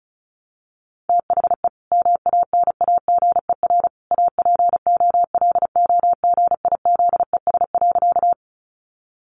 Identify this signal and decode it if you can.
Morse code, heard as a keyed tone, about 34 wpm, 700 Hz. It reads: THE MANAGER APOLOGIZES.
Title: Additional Easy Phrases 35wpm